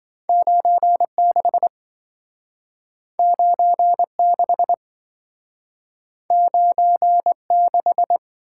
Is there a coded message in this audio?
Yes: Morse code from 0.3 to 8.2 s, heard as a keyed tone, about 28 wpm, 700 Hz. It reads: 96 96 96